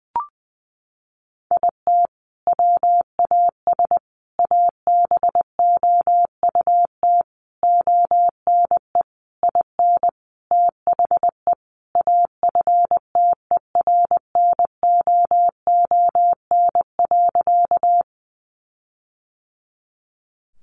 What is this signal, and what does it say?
Morse code, heard as a keyed tone, about 20 wpm, 700 Hz. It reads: IT WAS ABOUT ONE IN THE AFTERNOON.
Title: Sentences from Top 1500 Words 20wpm